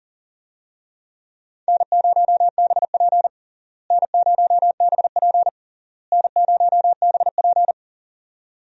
Morse code, heard as a keyed tone, about 40 wpm, 700 Hz. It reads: N0BP N0BP N0BP
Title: USA Call Signs - 4 Characters - Repeated 3x - 40wpm